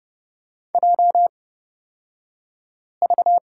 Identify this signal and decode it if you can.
Morse code, heard as a keyed tone, about 30 wpm, spaced as for 5 wpm, 690 Hz. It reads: JV
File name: Lesson 36d - BT - Two Characters 30wpm